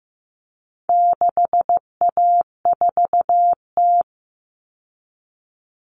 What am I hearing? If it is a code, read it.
Morse code, heard as a keyed tone, about 15 wpm, 700 Hz. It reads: 6A4T
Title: All Cut Numbers - 4-Digit 15wpm